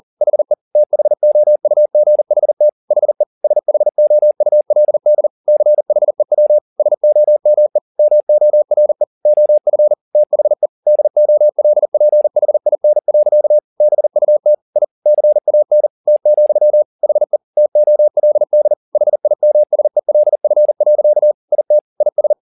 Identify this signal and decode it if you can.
Morse code, heard as a keyed tone, about 40 wpm, 600 Hz. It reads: HE THOUGHT HE SHOULD CHEW SOME MORE OF THE DOLPHIN. BUT I CAN T, HE TOLD HIMSELF. IT IS
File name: the old man and the sea 2 40wpm